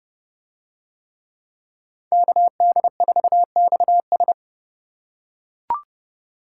Morse code, heard as a keyed tone, about 28 wpm, 700 Hz. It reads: KD4XS